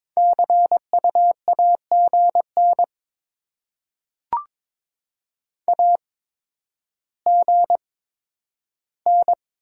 Morse code, heard as a keyed tone, about 22 wpm, spaced as for 11 wpm, 700 Hz. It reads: CU AGN AGN
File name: Top 20 QSO Elements - 8x Character Spacing - 22wpm